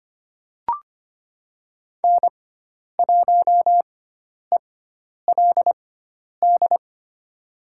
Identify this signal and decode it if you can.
Morse code, heard as a keyed tone, about 25 wpm, spaced as for 10 wpm, 700 Hz. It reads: N1ELD
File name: Lesson 16g - Letter c - Call Signs 25wpm